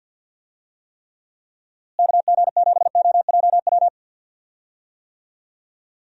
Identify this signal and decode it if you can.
Morse code, heard as a keyed tone, about 50 wpm, 700 Hz. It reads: KG7OJW
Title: USA Call Signs - 6 Characters - Repeated 2x - 50wpm